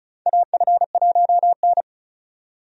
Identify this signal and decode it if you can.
Morse code, heard as a keyed tone, about 36 wpm, 700 Hz. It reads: AF1N
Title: USA Call Signs - 4 Characters - 35wpm